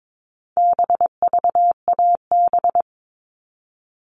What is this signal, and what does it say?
Morse code, heard as a keyed tone, about 22 wpm, 700 Hz. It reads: BVAB
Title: All Cut Numbers - 4-Digit 22wpm